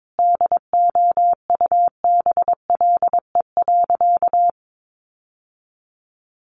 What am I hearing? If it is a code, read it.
Morse code, heard as a keyed tone, about 22 wpm, 700 Hz. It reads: DOUBLE.
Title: Sentences from Top 1400 Words 22wpm